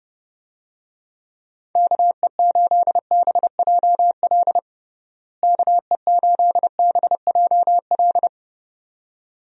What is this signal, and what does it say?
Morse code, heard as a keyed tone, about 30 wpm, 700 Hz. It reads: KE8BJL KE8BJL
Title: USA Call Signs - 6 Characters - Repeated 2x - 30wpm